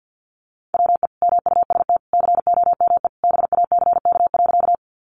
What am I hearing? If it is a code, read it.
Morse code, heard as a keyed tone, about 50 wpm, 700 Hz. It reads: WE MUST COME BACK.
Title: Sentences from Top 200 Words 50wpm